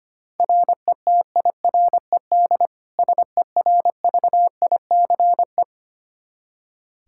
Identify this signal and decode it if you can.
Morse code, heard as a keyed tone, about 25 wpm, 700 Hz. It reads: RETIRED SERVICE
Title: Practice QSO - Exchange 3 - More Info - 25wpm